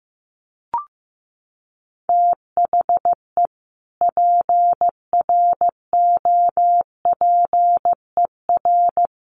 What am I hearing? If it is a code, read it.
Morse code, heard as a keyed tone, about 15 wpm, 700 Hz. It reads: THE PROPER
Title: Sentences from Top 1400 Words 15wpm